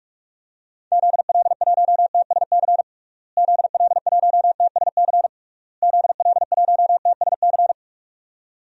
Morse code, heard as a keyed tone, about 46 wpm, 700 Hz. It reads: ZL1TSC ZL1TSC ZL1TSC
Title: DXCC-100 Call Signs - 6 Characters - Repeated 3x - 45wpm